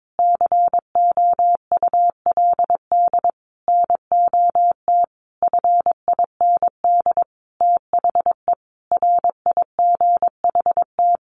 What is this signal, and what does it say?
Morse code, heard as a keyed tone, about 22 wpm, 700 Hz. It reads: COULD NOT FIND THE RIGHT
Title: Sentences from Top 600 Words 22wpm